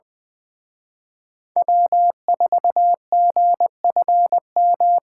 Morse code, heard as a keyed tone, about 23 wpm, 700 Hz. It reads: W4GFM
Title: Speed Racing - USA 1x3 Call Signs 20wpm